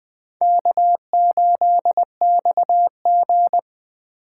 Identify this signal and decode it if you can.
Morse code, heard as a keyed tone, about 20 wpm, 700 Hz. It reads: K8XG
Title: Speed Racing - USA 1x2 Call Signs 17wpm